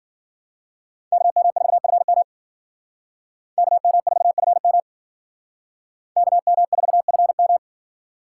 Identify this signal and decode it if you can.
Morse code, heard as a keyed tone, about 50 wpm, 700 Hz. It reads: KM4FM KM4FM KM4FM
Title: Speed Racing - USA Call Signs 40wpm